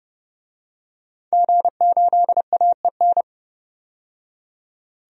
Morse code, heard as a keyed tone, about 30 wpm, 700 Hz. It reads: G8AEN
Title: DXCC-100 Call Signs - 5 Characters - 30wpm